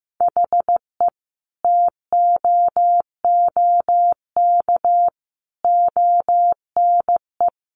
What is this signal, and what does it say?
Morse code, heard as a keyed tone, about 14 wpm, 700 Hz. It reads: HE TOOK ONE
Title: Sentences from Top 600 Words 15wpm